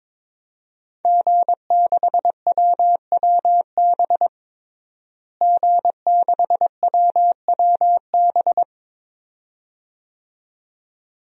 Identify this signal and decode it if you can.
Morse code, heard as a keyed tone, about 22 wpm, 700 Hz. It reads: G6WWB G6WWB
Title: DXCC-100 Call Signs - 5 Characters - Repeated 2x - 22wpm